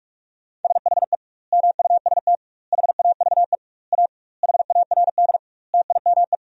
Morse code, heard as a keyed tone, about 44 wpm, 700 Hz. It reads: SHE MUST HAVE A HARD TIME
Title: 2x Word-Spacing - Sentences from Top 300 Words - 45wpm